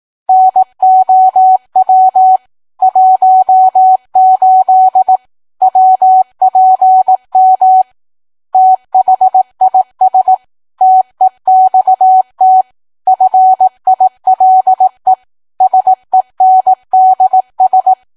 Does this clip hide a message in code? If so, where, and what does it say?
Yes: Morse code from 0.3 to 17.9 s, heard as a keyed tone, about 18 wpm, 750 Hz. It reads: NOW 18 WPM THIS TEXT FILE SENDS